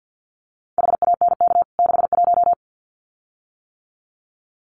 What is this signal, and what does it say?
Morse code, heard as a keyed tone, about 50 wpm, 700 Hz. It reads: HANK 61
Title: CWT Exchange 50wpm